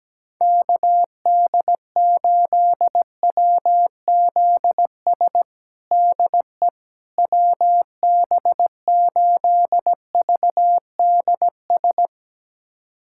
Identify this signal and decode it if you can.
Morse code, heard as a keyed tone, about 17 wpm, 700 Hz. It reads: KD8WZS DE WB8VDS
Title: USA Callsign DE Callsign - 6 Characters - 17wpm